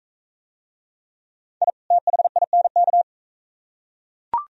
Morse code, heard as a keyed tone, about 40 wpm, 690 Hz. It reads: I THINK